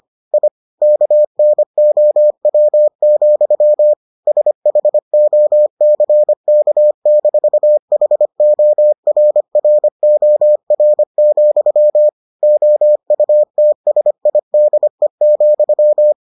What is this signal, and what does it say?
Morse code, heard as a keyed tone, about 25 wpm, 600 Hz. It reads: I KNOW, SHOCK-HORROR, OUTSIDE,